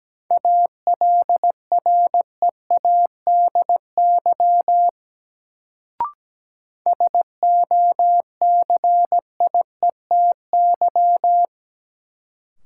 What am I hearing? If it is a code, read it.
Morse code, heard as a keyed tone, about 17 wpm, 700 Hz. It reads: ALREADY SOCIETY
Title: Sentences from Top 700 Words - New Word Drill - 17wpm